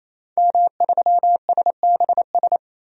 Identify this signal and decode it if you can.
Morse code, heard as a keyed tone, about 28 wpm, 700 Hz. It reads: M3SBS
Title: DXCC-100 Call Signs - 5 Characters - 28wpm